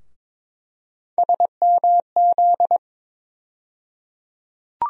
Morse code, heard as a keyed tone, about 22 wpm, 700 Hz. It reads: SMZ
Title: USA Call Sign Postfixes - 3 Letters 22wpm